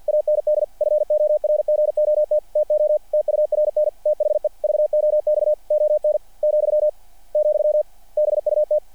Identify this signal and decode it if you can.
Morse code, heard as a keyed tone, about 48 wpm, 600 Hz. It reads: AND FORGOT TO TURN THE VOX ON , , BUT
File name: archived / CRSnet